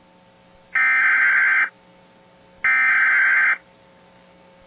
Emergency Alert System (SAME) tones (header bursts), heard as harsh buzzing bursts. It is heard from 0.8 to 3.5 s.